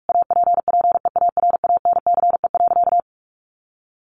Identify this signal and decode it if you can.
Morse code, heard as a keyed tone, about 44 wpm, 700 Hz. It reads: APPEARANCE.
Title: Sentences from Top 1100 Words 45wpm